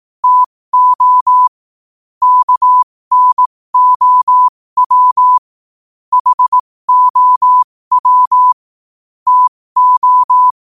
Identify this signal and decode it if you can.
Morse code, heard as a keyed tone, about 18 wpm, spaced as for 15 wpm, 1000 Hz. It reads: TO KNOW HOW TO
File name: qotdfiles / 15wpm